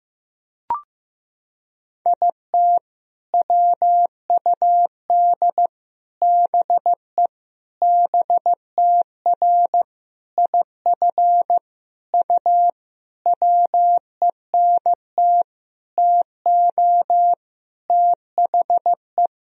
Morse code, heard as a keyed tone, about 15 wpm, 700 Hz. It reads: IT WUD BE BTR IF U WENT TO THE
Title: Sentences from Top 400 Words and Abbreviations 15wpm